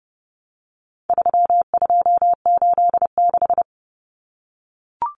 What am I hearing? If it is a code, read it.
Morse code, heard as a keyed tone, about 30 wpm, 700 Hz. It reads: 3286
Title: Common Cut Numbers - 4-Digit 30wpm